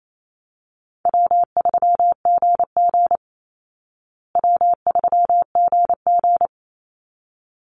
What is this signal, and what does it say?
Morse code, heard as a keyed tone, about 28 wpm, 700 Hz. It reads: W3GG W3GG
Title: Easy Call Signs Repeated 2x - 28wpm